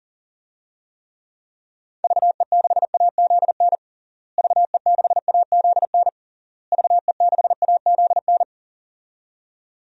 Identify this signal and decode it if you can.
Morse code, heard as a keyed tone, about 40 wpm, 700 Hz. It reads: VE6AZN VE6AZN VE6AZN
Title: DXCC-100 Call Signs - 6 Characters - Repeated 3x - 40wpm